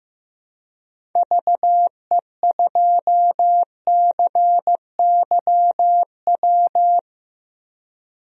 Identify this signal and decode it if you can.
Morse code, heard as a keyed tone, about 15 wpm, 700 Hz. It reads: VE2CYW
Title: DXCC-100 Call Signs - 6 Characters - Repeated 3x - 15wpm